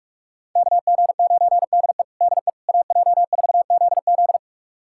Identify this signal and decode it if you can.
Morse code, heard as a keyed tone, about 46 wpm, 700 Hz. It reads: KG9DE DE AJ4ZZ
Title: USA Callsign DE Callsign - 5 Characters - 45wpm